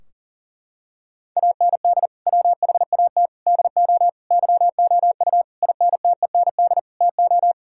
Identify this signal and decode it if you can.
Morse code, heard as a keyed tone, about 40 wpm, 700 Hz. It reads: AND WHAT DO YOU INTEND TO
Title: Sentences from Top 1600 Words 40wpm